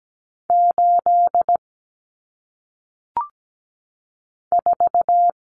Morse code, heard as a keyed tone, about 17 wpm, 700 Hz. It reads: 8 4
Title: Numbers - 1-Digit 17wpm